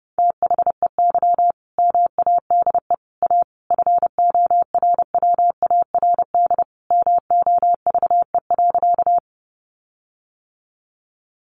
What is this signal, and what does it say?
Morse code, heard as a keyed tone, about 30 wpm, 700 Hz. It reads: THEY MADE A FORWARD MOVE.
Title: Sentences from Top 1300 Words 30wpm